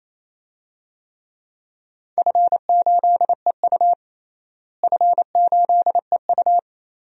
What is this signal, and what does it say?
Morse code, heard as a keyed tone, about 28 wpm, 700 Hz. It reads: F8EU F8EU
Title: DXCC-100 Call Signs - 4 Characters - Repeated 2x - 28wpm